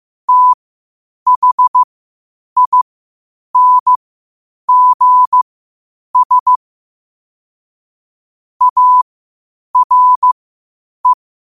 Morse code, heard as a keyed tone, about 15 wpm, spaced as for 8 wpm, 1000 Hz. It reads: THINGS ARE
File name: qotdfiles / 8wpm